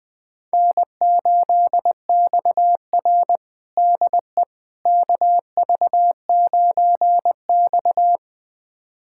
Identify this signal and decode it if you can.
Morse code, heard as a keyed tone, about 20 wpm, 700 Hz. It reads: N8XR DE KV9X
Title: USA Callsign DE Callsign - 4 Characters - 20wpm